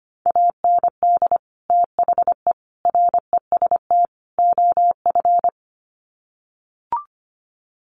Morse code, heard as a keyed tone, about 25 wpm, 700 Hz. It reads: AND THE REST OF
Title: Sets of 4 Words 25wpm